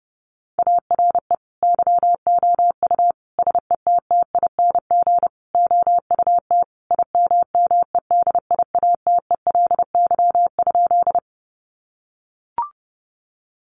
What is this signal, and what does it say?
Morse code, heard as a keyed tone, about 30 wpm, 700 Hz. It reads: ARE YOU SETTING OUT IMMEDIATELY?
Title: Sentences from Top 1300 Words 30wpm